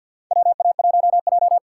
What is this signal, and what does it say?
Morse code, heard as a keyed tone, about 48 wpm, 700 Hz. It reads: WA1J
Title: USA Call Signs - 4 Characters - 3x Character Spacing - 50wpm